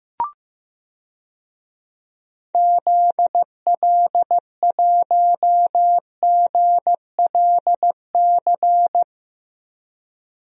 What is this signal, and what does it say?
Morse code, heard as a keyed tone, about 15 wpm, 700 Hz. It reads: ZL1GLC
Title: DXCC-100 Call Signs - 6 Characters - 15wpm